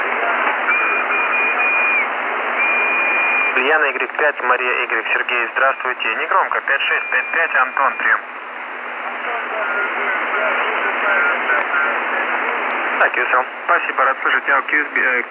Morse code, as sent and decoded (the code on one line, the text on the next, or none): none